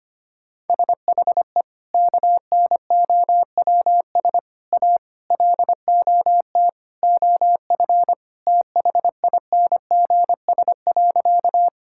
Morse code, heard as a keyed tone, about 25 wpm, 700 Hz. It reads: SHE KNOWS A LOT OF THINGS.
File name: practice / 25wpm / Sentences with 4-Word Phrases - 25wpm